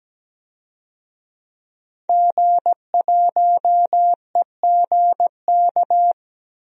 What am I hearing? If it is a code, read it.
Morse code, heard as a keyed tone, about 17 wpm, 700 Hz. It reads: G1EGK
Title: DXCC-100 Call Signs - 5 Characters - 17wpm